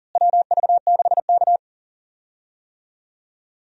Morse code, heard as a keyed tone, about 40 wpm, 700 Hz. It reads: WV6K
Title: USA Call Signs - 4 Characters - Repeated 2x - 40wpm